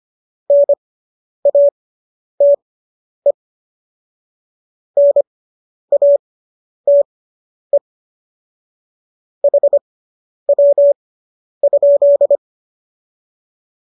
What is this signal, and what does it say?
Morse code, heard as a keyed tone, about 25 wpm, spaced as for 10 wpm, 570 Hz. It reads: NATE NATE HW?